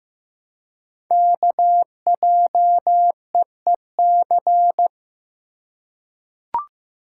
Morse code, heard as a keyed tone, about 15 wpm, 700 Hz. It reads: KJEEC
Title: Code Groups - 5 Letters - 15wpm